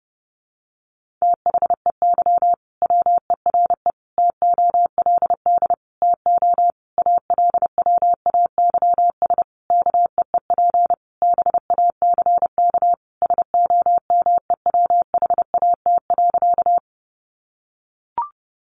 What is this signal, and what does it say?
Morse code, heard as a keyed tone, about 30 wpm, 700 Hz. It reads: THEY WERE TOLD TO ALWAYS KEEP BACK SOMEWHAT.
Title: Sentences from Top 1300 Words 30wpm